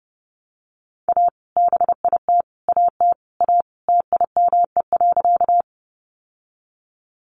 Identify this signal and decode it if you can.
Morse code, heard as a keyed tone, about 30 wpm, 700 Hz. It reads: A BIT AT A TIME.
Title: Sentences from Top 900 Words 30wpm